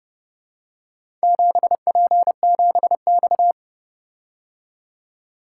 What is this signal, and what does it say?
Morse code, heard as a keyed tone, about 30 wpm, 700 Hz. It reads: 7P7X